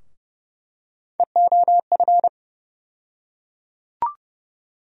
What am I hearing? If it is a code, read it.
Morse code, heard as a keyed tone, about 30 wpm, 700 Hz. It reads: EOF
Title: USA Call Sign Postfixes - 3 Letters 30wpm